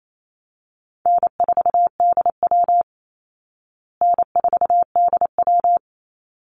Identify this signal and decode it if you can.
Morse code, heard as a keyed tone, about 28 wpm, 700 Hz. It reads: N4DW N4DW
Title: Call Signs Repeated 3x - 28wpm